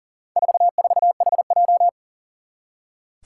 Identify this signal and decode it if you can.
Morse code, heard as a keyed tone, about 40 wpm, 700 Hz. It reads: 44HJ
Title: Four Challenging Letter-Number Combinations 40wpm